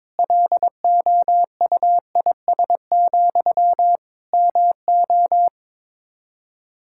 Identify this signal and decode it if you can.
Morse code, heard as a keyed tone, about 22 wpm, 700 Hz. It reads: LOUIS, MO
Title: Practice QSO - Exchange 1 - Beginning - 22wpm